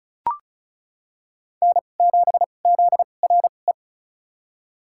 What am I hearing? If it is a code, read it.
Morse code, heard as a keyed tone, about 34 wpm, spaced as for 25 wpm, 700 Hz. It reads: N7ZRE
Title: USA Call Signs - 5 Characters - 2x Character Spacing - 35wpm